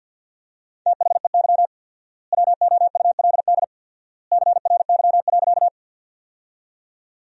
Morse code, heard as a keyed tone, about 51 wpm, 700 Hz. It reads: THEY WOULD CRY.
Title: 4x Word-Spacing - Sentences from Top 500 Words - 50wpm